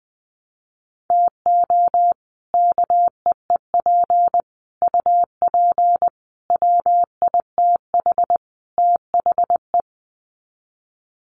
Morse code, heard as a keyed tone, about 20 wpm, 700 Hz. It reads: TO KEEP UP WITH THE
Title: Sets of 5 Words Encore 20wpm